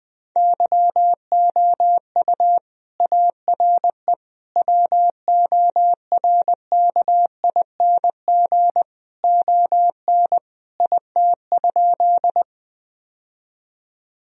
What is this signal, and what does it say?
Morse code, heard as a keyed tone, about 21 wpm, 700 Hz. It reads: YOU ARE WORKING ON IT?
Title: Sentences from Top 1000 Words - Review Part 4 - 20wpm